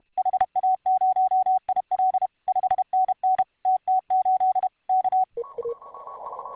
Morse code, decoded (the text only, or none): HA0IL 5NN TT8 K